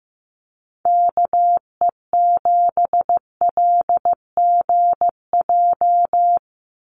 Morse code, heard as a keyed tone, about 15 wpm, 700 Hz. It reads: KE7LGJ
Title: USA Call Signs 15wpm